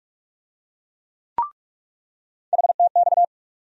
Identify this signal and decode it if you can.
Morse code, heard as a keyed tone, about 46 wpm, 700 Hz. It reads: HTX
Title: USA Call Sign Postfixes - 3 Letters 45wpm